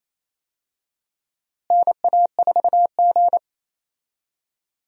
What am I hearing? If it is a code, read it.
Morse code, heard as a keyed tone, about 28 wpm, 700 Hz. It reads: NA4G
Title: USA Call Signs - 4 Characters - Repeated 3x - 28wpm